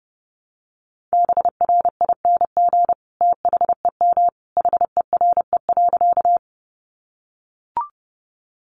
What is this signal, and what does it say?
Morse code, heard as a keyed tone, about 30 wpm, 700 Hz. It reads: BRING THEM HERE.
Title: Sentences from Top 500 Words 30wpm